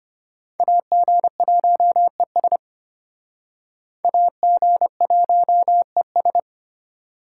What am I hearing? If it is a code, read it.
Morse code, heard as a keyed tone, about 34 wpm, 700 Hz. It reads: AG1ES AG1ES